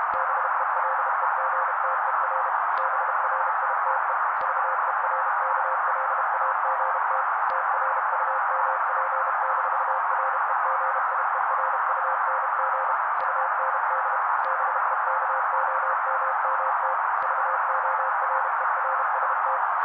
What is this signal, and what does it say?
Morse code, heard as a keyed tone, about 32 wpm, 530 Hz. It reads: BELEGTER BRITE VERKLAGT TRUMP-REGIERUNG UND BEKOMMT VORERST
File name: daily / 31wpm noise